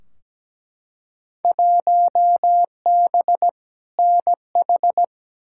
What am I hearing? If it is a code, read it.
Morse code, heard as a keyed tone, about 17 wpm, 700 Hz. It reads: 1B NH
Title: Field Day Exchanges Encore - 17wpm